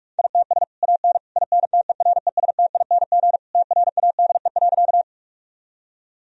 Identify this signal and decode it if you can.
Morse code, heard as a keyed tone, about 41 wpm, 700 Hz. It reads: ITS AN INTERESTING TRADE.
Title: Sentences from Top 1000 Words - Review Part 4 - 45wpm